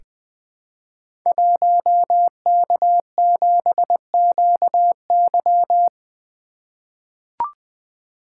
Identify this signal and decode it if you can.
Morse code, heard as a keyed tone, about 20 wpm, 700 Hz. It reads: 1K7QY
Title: Code Groups - 5 Letter:Numbers - 20wpm